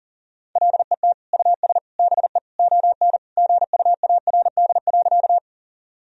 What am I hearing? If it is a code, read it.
Morse code, heard as a keyed tone, about 37 wpm, 700 Hz. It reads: LET US BE ON GUARD.
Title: Sentences from Top 1000 Words - Review Part 4 - 40wpm